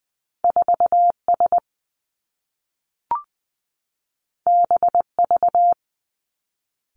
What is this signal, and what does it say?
Morse code, heard as a keyed tone, about 20 wpm, 700 Hz. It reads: 4 S B V
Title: Two Challenging Letter-Number Combinations 20wpm